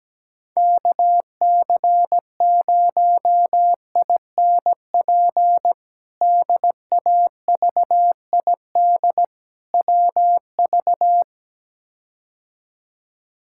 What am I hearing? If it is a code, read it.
Morse code, heard as a keyed tone, about 17 wpm, 700 Hz. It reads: KC0INP DAVID WV
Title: K1USN SST QSO Practice - 17wpm